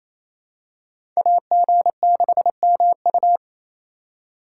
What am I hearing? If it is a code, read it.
Morse code, heard as a keyed tone, about 28 wpm, 700 Hz. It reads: AG6MU
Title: USA Call Signs - 2x Character Spacing - 28wpm